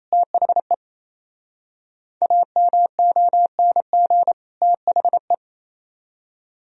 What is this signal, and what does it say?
Morse code, heard as a keyed tone, about 33 wpm, 700 Hz. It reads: THE AMONG THE
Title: Speed-Racing - Sets of 2 Words 28wpm